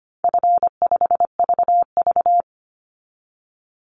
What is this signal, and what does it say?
Morse code, heard as a keyed tone, about 25 wpm, 700 Hz. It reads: F5VV
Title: Call Signs - 25wpm